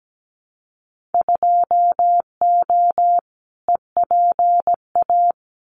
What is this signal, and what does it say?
Morse code, heard as a keyed tone, about 17 wpm, 700 Hz. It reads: 2O EPA
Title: Winter Field Day Exchanges Encore 17wpm